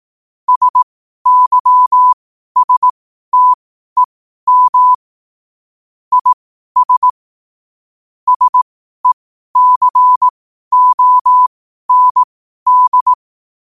Morse code, heard as a keyed tone, about 18 wpm, spaced as for 12 wpm, 1000 Hz. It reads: SYSTEM IS SECOND
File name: qotdfiles / 12wpm